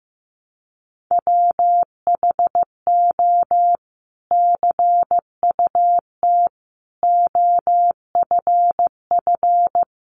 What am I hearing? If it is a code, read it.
Morse code, heard as a keyed tone, about 15 wpm, 700 Hz. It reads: WHO CUT OFF
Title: Sentences from Top 1300 Words 15wpm